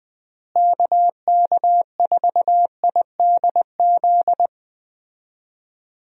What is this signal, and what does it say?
Morse code, heard as a keyed tone, about 20 wpm, 700 Hz. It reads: KK4IDZ
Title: USA Call Signs - 6 Characters - Repeated 2x - 20wpm